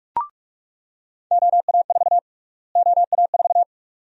Morse code, heard as a keyed tone, about 46 wpm, 700 Hz. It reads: OA4 OA4
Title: USA Call Sign Prefixes - Repeated 2x - 45wpm